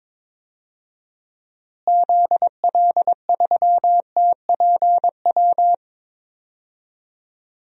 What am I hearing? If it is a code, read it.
Morse code, heard as a keyed tone, about 22 wpm, 700 Hz. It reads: ZL3TPW
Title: DXCC-100 Call Signs - 6 Characters - Repeated 3x - 22wpm